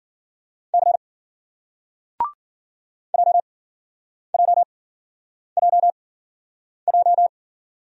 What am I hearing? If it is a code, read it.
Morse code, heard as a keyed tone, about 54 wpm, spaced as for 9 wpm, 700 Hz. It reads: C JJJJ